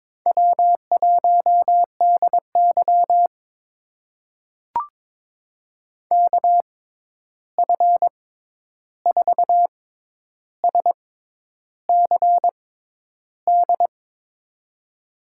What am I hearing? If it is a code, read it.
Morse code, heard as a keyed tone, about 22 wpm, spaced as for 10 wpm, 700 Hz. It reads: W1DY KF4SCD
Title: USA Call Signs - 6x Character Spacing - 22wpm